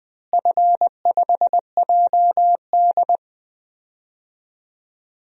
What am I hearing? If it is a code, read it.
Morse code, heard as a keyed tone, about 21 wpm, 700 Hz. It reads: F5JD
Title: DXCC-100 Call Signs - 4 Characters - Repeated 3x - 20wpm